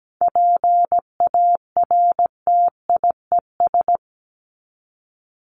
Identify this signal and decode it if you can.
Morse code, heard as a keyed tone, about 17 wpm, 700 Hz. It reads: PARTIES
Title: ES:S Suffix - Top 100 Words with Base Words 17wpm